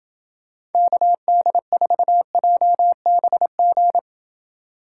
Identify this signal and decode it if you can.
Morse code, heard as a keyed tone, about 29 wpm, 700 Hz. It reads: KD4JBG